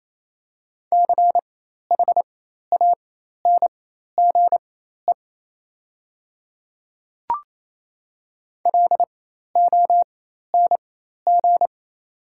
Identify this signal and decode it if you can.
Morse code, heard as a keyed tone, about 28 wpm, spaced as for 13 wpm, 690 Hz. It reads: CHANGE LONG
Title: 4x Character Spacing - Top 200 Words - 28wpm